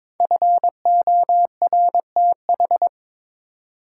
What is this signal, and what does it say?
Morse code, heard as a keyed tone, about 22 wpm, 700 Hz. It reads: FORTH
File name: 5 Letter Words - First 3 Letters Given - 22wpm